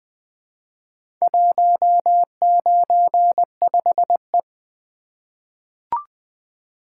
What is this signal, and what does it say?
Morse code, heard as a keyed tone, about 20 wpm, 700 Hz. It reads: 195E
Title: Four Letter-Number Combinations - 20wpm